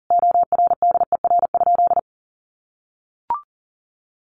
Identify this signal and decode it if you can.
Morse code, heard as a keyed tone, about 40 wpm, 700 Hz. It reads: ORDER?
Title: Sentences from Top 400 Words 40wpm